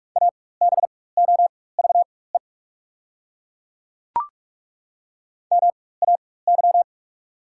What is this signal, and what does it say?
Morse code, heard as a keyed tone, about 42 wpm, spaced as for 21 wpm, 690 Hz. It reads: ABOVE MAY
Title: 4x Character Spacing - Top 500 Words - 45wpm